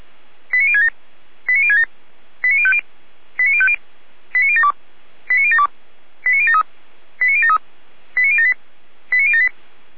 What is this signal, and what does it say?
Five-tone selective call (ZVEI), heard as quick runs of five tones, tones of about 70 ms. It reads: 89067 89067 89050 89050 89072 89072 89073 89073 89078 89078
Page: ZVEI